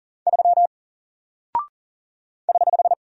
Morse code, heard as a keyed tone, about 40 wpm, 700 Hz. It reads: 3 <HH>
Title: Single Character (Letter:Number:Punctuation:Prosign) Alt 40wpm